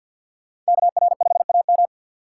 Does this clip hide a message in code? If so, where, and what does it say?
Yes: Morse code from 0.7 to 1.9 s, heard as a keyed tone, about 63 wpm, 700 Hz. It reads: KR5AM